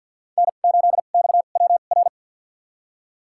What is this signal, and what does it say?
Morse code, heard as a keyed tone, about 53 wpm, spaced as for 37 wpm, 690 Hz. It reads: N8XWR
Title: USA Call Signs - 2x Character Spacing - 50wpm